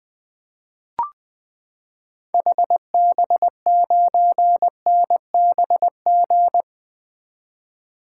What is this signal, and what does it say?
Morse code, heard as a keyed tone, about 20 wpm, 700 Hz. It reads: HB9NBG
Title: 2023 SOTA Activator Call Signs - 20wpm